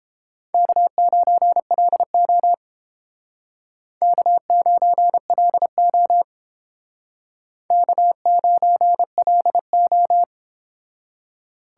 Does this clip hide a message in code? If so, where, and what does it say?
Yes: Morse code from 0.5 to 10.2 s, heard as a keyed tone, about 31 wpm, 700 Hz. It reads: K9LO K9LO K9LO